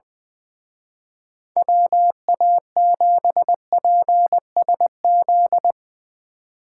Morse code, heard as a keyed tone, about 21 wpm, 700 Hz. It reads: WA7PSZ
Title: Speed Racing - USA 2x3 Call Signs 20wpm